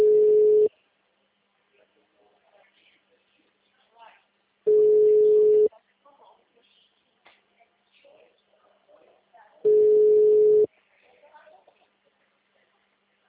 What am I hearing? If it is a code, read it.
Ringback tone (Europe, 425 Hz), with speech.
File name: out- -2013- - - .26856